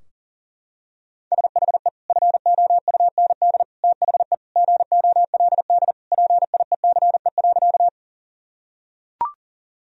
Morse code, heard as a keyed tone, about 40 wpm, 700 Hz. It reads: SHE FOUND THE GOLD PIECE.
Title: Sentences from Top 1000 Words - Review Part 3 - 40wpm